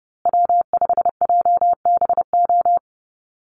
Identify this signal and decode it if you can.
Morse code, heard as a keyed tone, about 30 wpm, 700 Hz. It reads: W5JBO
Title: Call Signs Repeated 2x - 30wpm